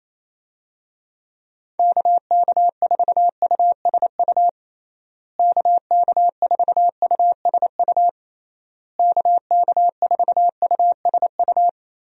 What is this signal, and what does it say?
Morse code, heard as a keyed tone, about 28 wpm, 700 Hz. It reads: KK4USU KK4USU KK4USU
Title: USA Call Signs - 6 Characters - Repeated 3x - 28wpm